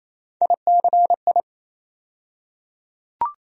Morse code, heard as a keyed tone, about 27 wpm, 700 Hz. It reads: ICI
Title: USA Call Sign Postfixes - 3 Letters 28wpm